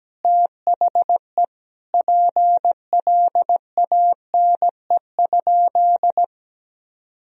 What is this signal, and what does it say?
Morse code, heard as a keyed tone, about 17 wpm, 700 Hz. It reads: THE PLANE?
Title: Sentences from Top 1000 Words - Review Part 3 - 17wpm